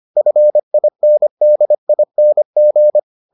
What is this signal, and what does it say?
Morse code, heard as a keyed tone, about 25 wpm, 600 Hz. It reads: FINDING